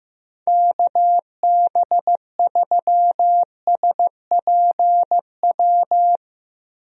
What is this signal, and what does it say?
Morse code, heard as a keyed tone, about 15 wpm, 700 Hz. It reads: KB3SPW
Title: USA Call Signs - 6 Characters - Repeated 3x - 15wpm